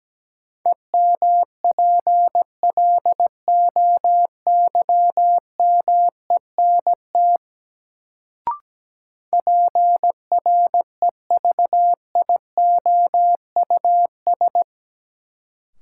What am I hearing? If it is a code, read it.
Morse code, heard as a keyed tone, about 17 wpm, 700 Hz. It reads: EMPLOYMENT PREVIOUS
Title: Sentences from Top 1500 Words - New Word Drill Alt - 17wpm